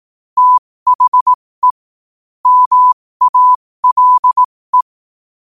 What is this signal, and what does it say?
Morse code, heard as a keyed tone, about 18 wpm, spaced as for 15 wpm, 1000 Hz. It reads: THE MALE
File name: qotdfiles / 15wpm